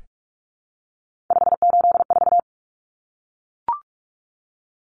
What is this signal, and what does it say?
Morse code, heard as a keyed tone, about 44 wpm, 700 Hz. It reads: 584
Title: Numbers - 3-Digit 45wpm